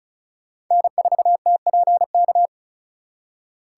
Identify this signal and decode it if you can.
Morse code, heard as a keyed tone, about 36 wpm, 700 Hz. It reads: N4TPK
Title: USA Call Signs - 2x Character Spacing - 35wpm